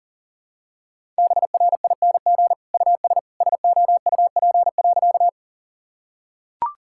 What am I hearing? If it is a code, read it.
Morse code, heard as a keyed tone, about 40 wpm, 700 Hz. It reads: BRING US SOUP.
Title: Sentences from Top 1000 Words - Review Part 6 - 40wpm